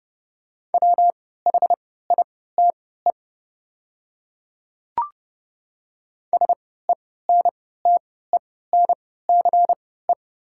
Morse code, heard as a keyed tone, about 30 wpm, spaced as for 17 wpm, 700 Hz. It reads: WHITE SENTENCE
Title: 3x Character Spacing - Top 500 Words - 30wpm